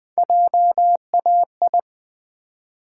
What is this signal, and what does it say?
Morse code, heard as a keyed tone, about 20 wpm, 700 Hz. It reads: JAI
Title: Build A Word - 1 Letter At A Time - 4-Letter Words - 20wpm